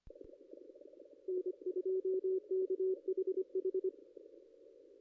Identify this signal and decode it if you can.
Morse code, heard as a keyed tone, about 23 wpm, 380 Hz. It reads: N2KHH
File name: contests / ss-cw-k5tr-op-n5rz -2200